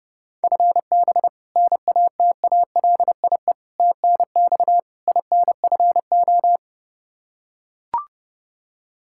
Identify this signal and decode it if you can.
Morse code, heard as a keyed tone, about 32 wpm, 700 Hz. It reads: FB NATALIE TNX INFO
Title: Practice QSO - Exchange 3 - More Info - 30wpm